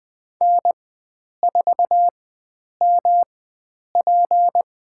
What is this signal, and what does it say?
Morse code, heard as a keyed tone, about 20 wpm, spaced as for 9 wpm, 700 Hz. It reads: N4MP
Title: USA Call Signs - 4 Characters - 4x Character Spacing - 20wpm